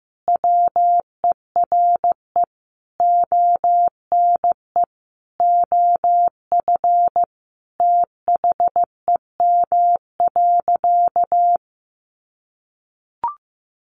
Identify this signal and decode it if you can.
Morse code, heard as a keyed tone, about 15 wpm, 700 Hz. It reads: WERE ONE OF THEM.
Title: Sentences from Top 1500 Words 15wpm